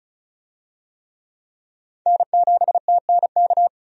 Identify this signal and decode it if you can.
Morse code, heard as a keyed tone, about 34 wpm, 700 Hz. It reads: N7TNK
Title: USA Call Signs - 5 Characters - 35wpm